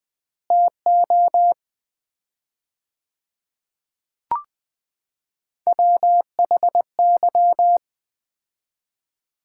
Morse code, heard as a keyed tone, about 21 wpm, 700 Hz. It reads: TO WHY
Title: 4x Word-Spacing - Sets of 3 Words - 20wpm